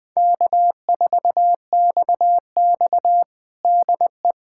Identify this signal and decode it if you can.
Morse code, heard as a keyed tone, about 20 wpm, 700 Hz. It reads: K4XX DE
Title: USA Callsign DE Callsign - 4 Characters - 20wpm